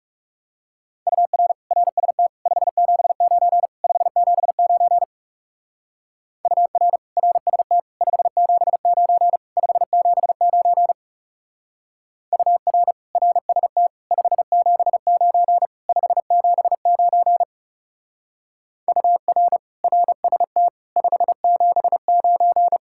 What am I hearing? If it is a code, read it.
Morse code, heard as a keyed tone, about 38 wpm, 700 Hz. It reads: UR RST 579 579 UR RST 579 579 UR RST 579 579 UR RST 579